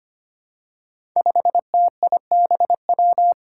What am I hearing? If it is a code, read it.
Morse code, heard as a keyed tone, about 25 wpm, 700 Hz. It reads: 5TIBW
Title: Five Challenging Letter-Number Combinations 25wpm